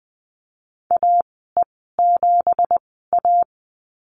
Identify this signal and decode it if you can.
Morse code, heard as a keyed tone, about 20 wpm, spaced as for 15 wpm, 700 Hz. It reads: AE7A
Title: 173 Lesson 28g - Number 9 - Call Signs 20wpm